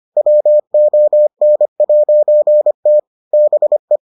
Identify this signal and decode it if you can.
Morse code, heard as a keyed tone, about 25 wpm, 600 Hz. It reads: WON'T BE